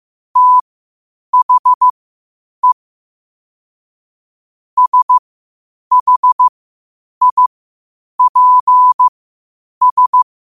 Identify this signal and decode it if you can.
Morse code, heard as a keyed tone, about 15 wpm, spaced as for 8 wpm, 1000 Hz. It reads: THE SHIPS
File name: qotdfiles / 8wpm